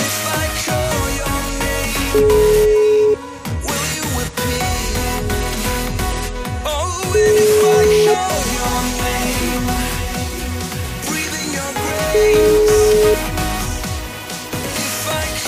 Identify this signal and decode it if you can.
Ringback tone (Europe, 425 Hz), with noise.